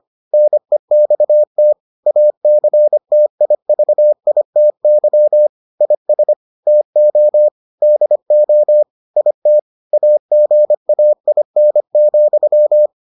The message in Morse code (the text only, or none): NEXT ACTIVITY IS TO DO IT AGAIN,